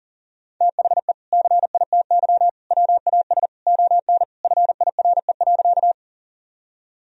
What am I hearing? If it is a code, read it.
Morse code, heard as a keyed tone, about 37 wpm, 700 Hz. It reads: THE CITY WAS ON FIRE.
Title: Sentences from Top 1000 Words - Review Part 1 - 40wpm